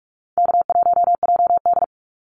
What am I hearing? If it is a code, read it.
Morse code, heard as a keyed tone, about 44 wpm, 700 Hz. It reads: K1JD
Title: NA SOTA Activator Callsigns 45wpm